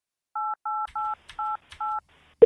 DTMF, heard as telephone touch tones, tones of about 210 ms. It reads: 88888